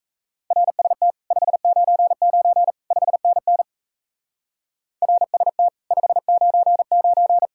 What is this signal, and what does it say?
Morse code, heard as a keyed tone, about 34 wpm, 700 Hz. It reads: RST 599 5NN RST 599
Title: Speed-Racing Common QSO Phrases Encore - 28wpm